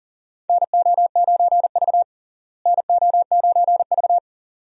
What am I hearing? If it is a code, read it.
Morse code, heard as a keyed tone, about 40 wpm, 700 Hz. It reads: NO9V NO9V
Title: USA Call Signs - 4 Characters - Repeated 2x - 40wpm